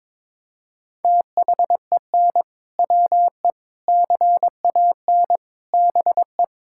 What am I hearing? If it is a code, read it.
Morse code, heard as a keyed tone, about 21 wpm, 700 Hz. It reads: THEN WE CAN BE
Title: Sentences from Top 1600 Words 22wpm